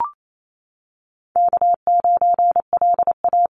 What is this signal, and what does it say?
Morse code, heard as a keyed tone, about 28 wpm, 700 Hz. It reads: K9LA
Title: Call Signs - 28wpm